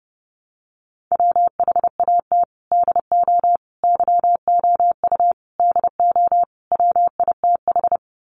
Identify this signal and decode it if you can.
Morse code, heard as a keyed tone, about 30 wpm, 700 Hz. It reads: WHAT DO YOU DO WITH
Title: Sets of 5 Words Encore2 30wpm